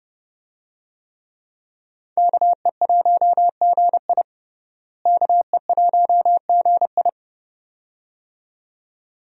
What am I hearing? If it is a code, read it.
Morse code, heard as a keyed tone, about 32 wpm, 700 Hz. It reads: KE1GI KE1GI
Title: USA Call Signs - 5 Characters - Repeated 2x - 30wpm